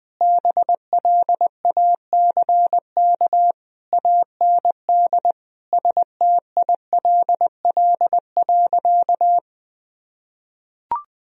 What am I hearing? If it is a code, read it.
Morse code, heard as a keyed tone, about 20 wpm, 700 Hz. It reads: BLACK AND STILL.
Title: Sentences from Top 1000 Words - Review Part 2 - 20wpm